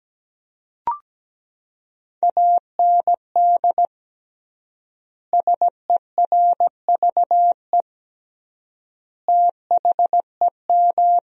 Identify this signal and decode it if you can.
Morse code, heard as a keyed tone, about 17 wpm, 700 Hz. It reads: AND SERVE THEM
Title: 3x Word-Spacing - Sentences from Top 400 Words - 17wpm